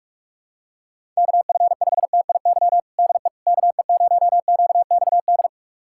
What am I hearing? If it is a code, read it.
Morse code, heard as a keyed tone, about 46 wpm, 700 Hz. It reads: KF5TIY DE KE0QXD
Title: USA Callsign DE Callsign - 6 Characters - 45wpm